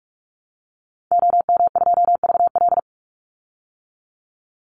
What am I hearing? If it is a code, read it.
Morse code, heard as a keyed tone, about 44 wpm, 700 Hz. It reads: OM2VL
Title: Call Signs Repeated 2x - 45wpm